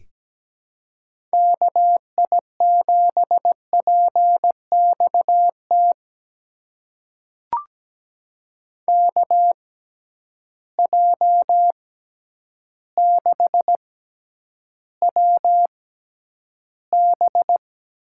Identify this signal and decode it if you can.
Morse code, heard as a keyed tone, about 17 wpm, spaced as for 9 wpm, 700 Hz. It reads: KI7PXT KJ6WB
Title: USA Call Signs - 6 Characters - 6x Character Spacing - 17wpm